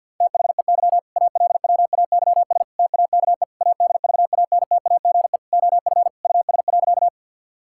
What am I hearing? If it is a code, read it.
Morse code, heard as a keyed tone, about 48 wpm, 700 Hz. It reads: THEY ALWAYS TAKE ADVANTAGE OF US.
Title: Sentences from Top 1000 Words - Review Part 6 - 50wpm